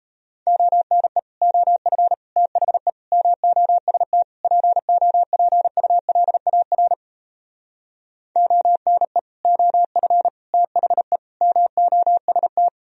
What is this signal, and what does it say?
Morse code, heard as a keyed tone, about 37 wpm, 700 Hz. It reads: ONE OF THE MOST POPULAR ONE OF THE MOST